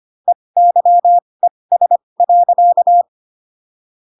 Morse code, heard as a keyed tone, about 25 wpm, spaced as for 20 wpm, 700 Hz. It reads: EYES.
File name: Pinocchio / Pinocchio ch6 15 25 ik1hge